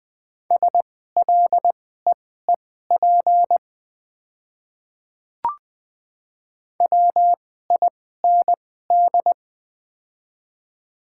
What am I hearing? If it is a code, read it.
Morse code, heard as a keyed tone, about 21 wpm, spaced as for 15 wpm, 690 Hz. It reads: SLEEP WIND
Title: 2x Character Spacing - Top 500 Words - 20wpm